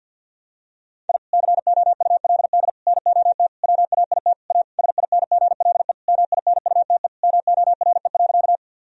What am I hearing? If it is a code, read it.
Morse code, heard as a keyed tone, about 50 wpm, 700 Hz. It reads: I COULD NOT WAIT A SINGLE MINUTE MORE.
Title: Sentences from Top 1000 Words - Review Part 4 - 50wpm